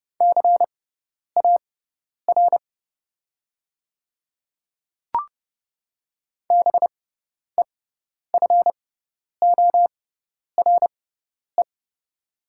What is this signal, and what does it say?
Morse code, heard as a keyed tone, about 30 wpm, spaced as for 10 wpm, 700 Hz. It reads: CAR BEFORE
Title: 6x Character Spacing - Top 300 Words - 30wpm